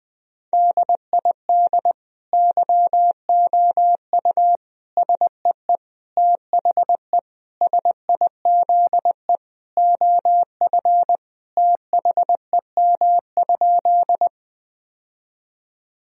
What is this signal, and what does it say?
Morse code, heard as a keyed tone, about 20 wpm, 700 Hz. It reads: DID YOU SEE THE SIZE OF THEM?
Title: Sentences from Top 1000 Words - Review Part 2 - 20wpm